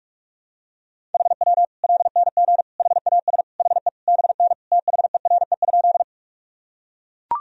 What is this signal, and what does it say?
Morse code, heard as a keyed tone, about 42 wpm, 700 Hz. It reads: HW LNG HAS HE BN THERE?
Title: Sentences from Top 100 Words and Abbreviations 45wpm